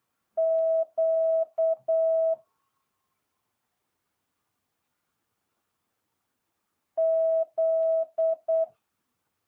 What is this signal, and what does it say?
Morse code, heard as a keyed tone, about 8 wpm, 650 Hz. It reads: Q Z
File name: jh8cbh